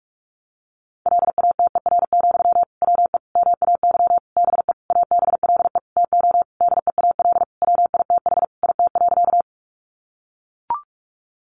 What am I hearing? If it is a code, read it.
Morse code, heard as a keyed tone, about 44 wpm, 700 Hz. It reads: LATER, WE MAY BE ABLE TO DEAL WITH IT.
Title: Sentences from Top 800 Words 45wpm